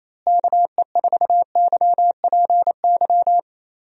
Morse code, heard as a keyed tone, about 28 wpm, 700 Hz. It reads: KE4YPY
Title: USA Call Signs - 6 Characters - Repeated 3x - 28wpm